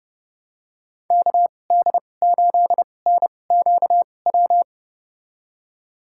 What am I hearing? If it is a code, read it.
Morse code, heard as a keyed tone, about 30 wpm, spaced as for 22 wpm, 700 Hz. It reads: KD8NQW
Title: USA Call Signs - 2x Character Spacing - 30wpm